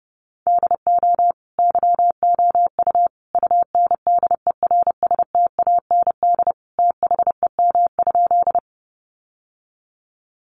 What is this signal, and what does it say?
Morse code, heard as a keyed tone, about 30 wpm, 700 Hz. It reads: DO YOU UNDERSTAND THEM?
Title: Sentences from Top 600 Words 30wpm